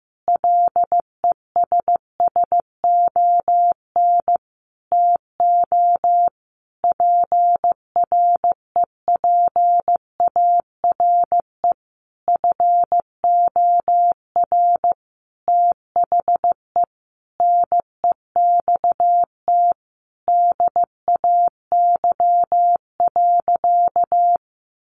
Morse code, heard as a keyed tone, about 15 wpm, 700 Hz. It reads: LESSON TO PREPARE FOR THE NEXT DAY.
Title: Sentences from Top 1200 Words 15wpm